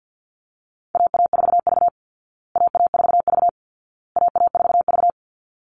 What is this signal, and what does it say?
Morse code, heard as a keyed tone, about 50 wpm, 700 Hz. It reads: AA4V AA4V AA4V
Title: Call Signs Repeated 2x - 50wpm